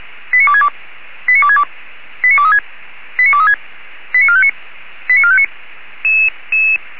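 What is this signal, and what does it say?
Five-tone selective call (ZVEI), heard as quick runs of five tones, tones of about 70 ms. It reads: 89262 89262 89236 89236 89459 89459
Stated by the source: ZVEI five-tone selective call received on SDR, code 89417.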